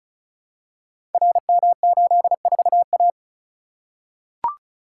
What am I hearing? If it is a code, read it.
Morse code, heard as a keyed tone, about 35 wpm, 700 Hz. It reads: RM84A
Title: Code Groups - 5 Letter:Numbers - 35wpm